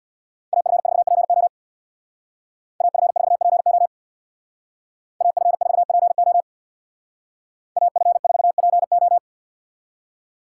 Morse code, heard as a keyed tone, about 62 wpm, 700 Hz. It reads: AF4PO AF4PO AF4PO AF4PO
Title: Speed Racing - USA Call Signs 50wpm